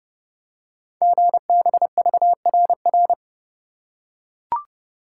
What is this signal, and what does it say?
Morse code, heard as a keyed tone, about 30 wpm, 700 Hz. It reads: GBVRR
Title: Code Groups - 5 Letters - 30wpm